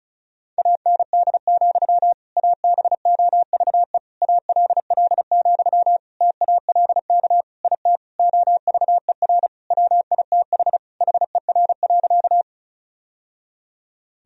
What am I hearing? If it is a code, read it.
Morse code, heard as a keyed tone, about 36 wpm, 700 Hz. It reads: AND, ABOVE ALL, TALK IT OVER WITH HER.
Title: Sentences from Top 1000 Words - Review Part 2 - 35wpm